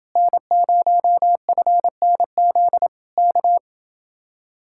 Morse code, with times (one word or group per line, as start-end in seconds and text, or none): 0.2-2.9 N0FNZ
3.2-3.6 K